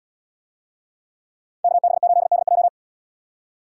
Morse code, heard as a keyed tone, about 73 wpm, 700 Hz. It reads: KB8DJ